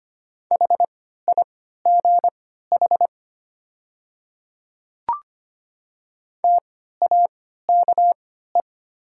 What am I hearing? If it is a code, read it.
Morse code, heard as a keyed tone, about 25 wpm, spaced as for 14 wpm, 700 Hz. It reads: HIGH TAKE
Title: 3x Character Spacing - Top 200 Words - 25wpm